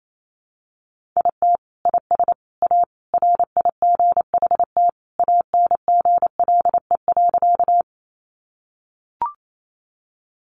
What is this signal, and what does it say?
Morse code, heard as a keyed tone, about 28 wpm, 700 Hz. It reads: IT IS A RIGHT ANGLE.
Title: Sentences from Top 1500 Words 28wpm